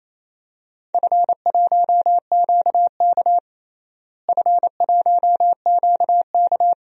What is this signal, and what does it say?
Morse code, heard as a keyed tone, about 28 wpm, 700 Hz. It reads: F1QK F1QK
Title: DXCC-100 Call Signs - 4 Characters - Repeated 2x - 28wpm